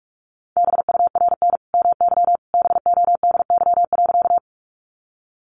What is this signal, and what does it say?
Morse code, heard as a keyed tone, about 44 wpm, 700 Hz. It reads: BURN MY BODY.
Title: Sentences from Top 1300 Words 45wpm